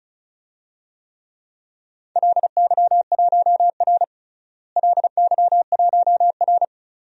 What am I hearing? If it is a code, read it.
Morse code, heard as a keyed tone, about 36 wpm, 700 Hz. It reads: LY1R LY1R
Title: DXCC-100 Call Signs - 4 Characters - Repeated 2x - 35wpm